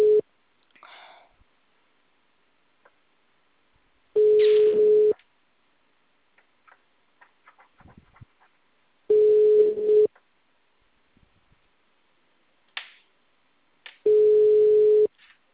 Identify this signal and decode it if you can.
Ringback tone (Europe, 425 Hz).
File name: out- -2015- - - .2857